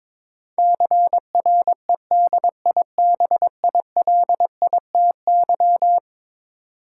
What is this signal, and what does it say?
Morse code, heard as a keyed tone, about 22 wpm, 700 Hz. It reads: CREDIBILITY
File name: practice / 22wpm / Eleven Letter Words - 22wpm